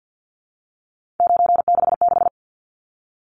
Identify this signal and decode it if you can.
Morse code, heard as a keyed tone, about 50 wpm, 700 Hz. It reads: 966
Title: Numbers - 3-Digit 50wpm